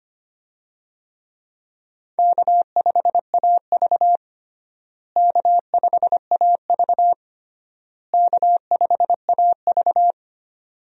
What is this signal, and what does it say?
Morse code, heard as a keyed tone, about 25 wpm, 700 Hz. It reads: K5AV K5AV K5AV
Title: USA Call Signs - 4 Characters - Repeated 3x - 25wpm